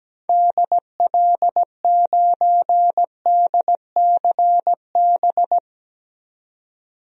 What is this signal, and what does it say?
Morse code, heard as a keyed tone, about 17 wpm, 700 Hz. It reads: DL9DCB
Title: DXCC-100 Call Signs - 6 Characters - 17wpm